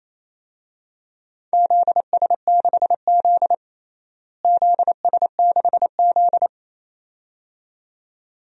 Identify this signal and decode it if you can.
Morse code, heard as a keyed tone, about 28 wpm, 700 Hz. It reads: ZS6Z ZS6Z
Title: DXCC-100 Call Signs - 4 Characters - Repeated 2x - 28wpm